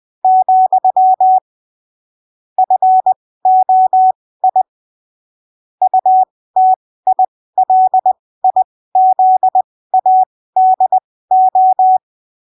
Morse code, heard as a keyed tone, about 20 wpm, spaced as for 15 wpm, 750 Hz. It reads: , FOI UTILIZADO
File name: arquivos / texto / texto 14 20ppm